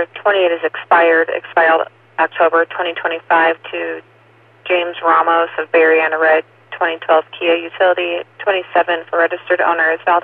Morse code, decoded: none

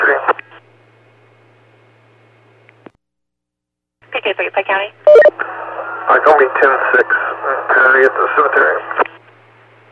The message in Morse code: none

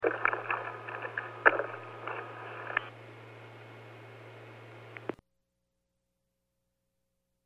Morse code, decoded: none

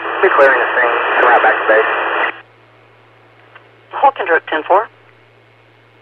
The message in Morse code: none

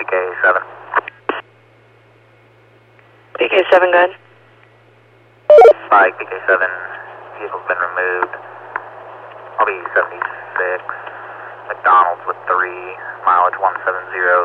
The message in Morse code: none

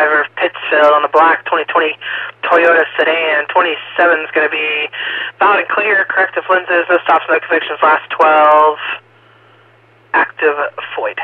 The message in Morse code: none

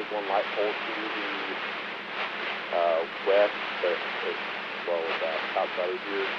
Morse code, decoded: none

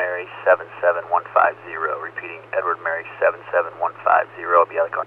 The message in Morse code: none

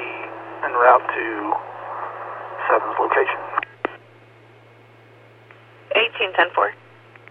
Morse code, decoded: none

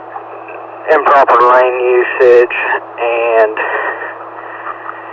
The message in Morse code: none

